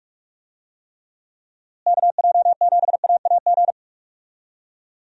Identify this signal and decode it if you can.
Morse code, heard as a keyed tone, about 46 wpm, 700 Hz. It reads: KJ7AAG
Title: USA Call Signs - 6 Characters - Repeated 2x - 45wpm